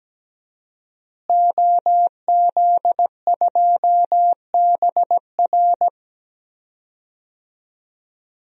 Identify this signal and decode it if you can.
Morse code, heard as a keyed tone, about 17 wpm, 700 Hz. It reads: OZ2BR
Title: DXCC-100 Call Signs - 5 Characters - Repeated 3x - 17wpm